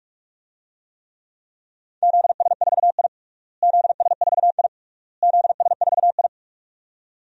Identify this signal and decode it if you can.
Morse code, heard as a keyed tone, about 46 wpm, 690 Hz. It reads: ZS4I ZS4I ZS4I
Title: DXCC-100 Call Signs - 4 Characters - Repeated 3x - 45wpm